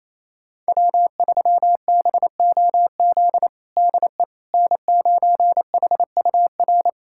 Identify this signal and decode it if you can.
Morse code, heard as a keyed tone, about 28 wpm, 700 Hz. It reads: W3BOZ DE N9HUR
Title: USA Callsign DE Callsign - 5 Characters - 28wpm